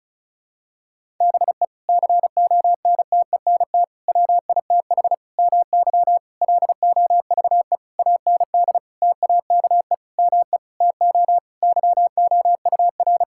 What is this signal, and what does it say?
Morse code, heard as a keyed tone, about 36 wpm, 700 Hz. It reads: BE CONTENT WITH MY LOVE AND TAKE ME TO YOUR
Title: Sentences from Top 1000 Words - Review Part 6 - 35wpm